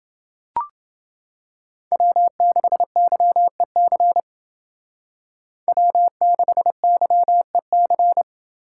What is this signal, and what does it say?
Morse code, heard as a keyed tone, about 28 wpm, 700 Hz. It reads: W6YEC W6YEC